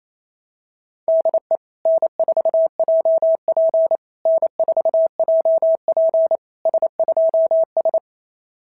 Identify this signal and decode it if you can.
Morse code, heard as a keyed tone, about 27 wpm, 660 Hz. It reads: DE N4JP N4JP S2S
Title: SOTA QSO Practice 28wpm